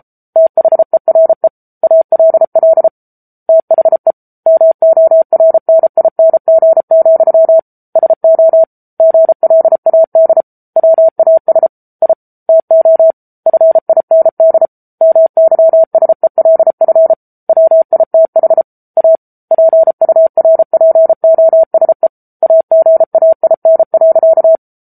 Morse code, heard as a keyed tone, about 34 wpm, 650 Hz. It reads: THERE ALL THE MORNING, SO GLAD WAS I TO FIND MYSELF WITH A PURPOSE AGAIN.
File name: Book Two / 07-Chapter 7 - 33 WPM